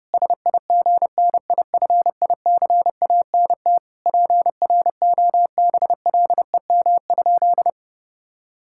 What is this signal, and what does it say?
Morse code, heard as a keyed tone, about 30 wpm, 700 Hz. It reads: SIGNIFICANT PROBLEM?
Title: Sentences from Top 1000 Words - Review Part 6 - 30wpm